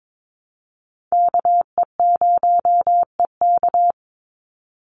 Morse code, heard as a keyed tone, about 22 wpm, 700 Hz. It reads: KE0EK
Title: Call Signs - 22wpm